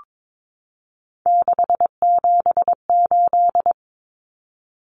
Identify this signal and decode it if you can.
Morse code, heard as a keyed tone, about 22 wpm, 700 Hz. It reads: 678
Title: Common Cut Numbers - 3-Digit 22wpm